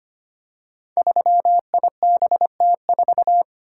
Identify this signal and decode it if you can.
Morse code, heard as a keyed tone, about 25 wpm, 700 Hz. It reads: 3IBT4
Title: Code Groups - 5 Letter:Numbers - 25wpm